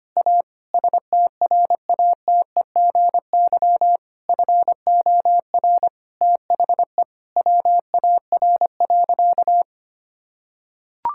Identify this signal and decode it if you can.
Morse code, heard as a keyed tone, about 25 wpm, 700 Hz. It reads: A STRATEGY FOR THE WAR.
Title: Sentences from Top 1000 Words - Review Part 5 - 25wpm